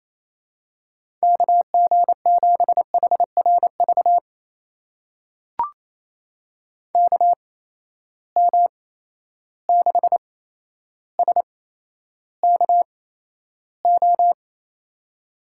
Morse code, heard as a keyed tone, about 28 wpm, spaced as for 12 wpm, 700 Hz. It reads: KG7HRV KM6SKO
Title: USA Call Signs - 6 Characters - 8x Character Spacing - 28wpm